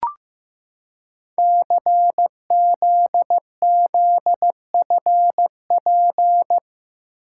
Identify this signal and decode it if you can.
Morse code, heard as a keyed tone, about 15 wpm, 700 Hz. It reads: CZZFP
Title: Code Groups - 5 Letter:Numbers - 15wpm